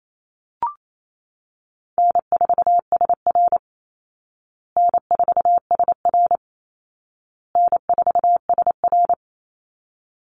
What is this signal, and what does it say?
Morse code, heard as a keyed tone, about 28 wpm, 700 Hz. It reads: N4SR N4SR N4SR
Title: Easy Call Signs Repeated 3x - 28wpm